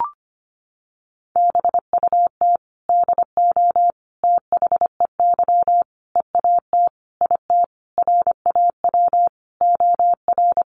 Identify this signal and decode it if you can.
Morse code, heard as a keyed tone, about 25 wpm, 700 Hz. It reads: BUT DO THEY EAT IT RAW OR
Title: Sentences from Top 1500 Words 25wpm